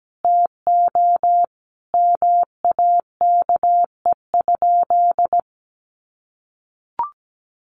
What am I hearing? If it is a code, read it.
Morse code, heard as a keyed tone, about 17 wpm, 700 Hz. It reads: TO MAKE?
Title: Sentences from Top 1400 Words 17wpm